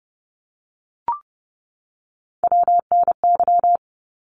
Morse code, heard as a keyed tone, about 30 wpm, 700 Hz. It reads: WNY
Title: ARRL Sections 30wpm